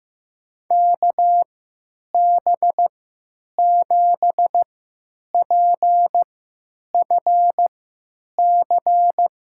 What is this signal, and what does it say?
Morse code, heard as a keyed tone, about 15 wpm, spaced as for 9 wpm, 700 Hz. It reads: KB7PFC
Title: USA Call Signs - 3x Character Spacing - 15wpm